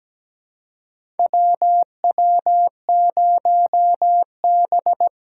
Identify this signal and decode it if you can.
Morse code, heard as a keyed tone, about 17 wpm, 700 Hz. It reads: WW0B
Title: USA Call Signs - 4 Characters - 2x Character Spacing - 17wpm